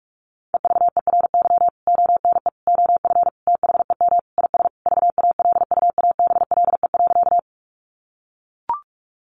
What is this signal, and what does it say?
Morse code, heard as a keyed tone, about 44 wpm, 700 Hz. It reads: EVERY ONE OF THEM IS VALUABLE.
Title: Sentences from Top 1100 Words 45wpm